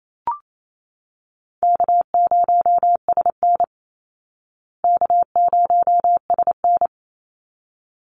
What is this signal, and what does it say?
Morse code, heard as a keyed tone, about 28 wpm, 700 Hz. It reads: K0SN K0SN
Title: Call Signs Repeated 3x - 28wpm